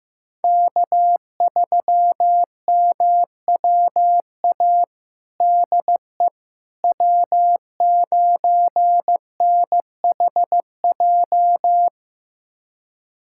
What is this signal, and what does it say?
Morse code, heard as a keyed tone, about 15 wpm, 700 Hz. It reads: K3MWA DE W9NHJ
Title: USA Callsign DE Callsign - 5 Characters - 15wpm